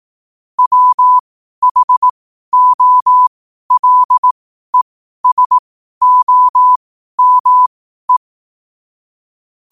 Morse code, heard as a keyed tone, about 18 wpm, spaced as for 12 wpm, 1000 Hz. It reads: WHOLESOME
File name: qotdfiles / 12wpm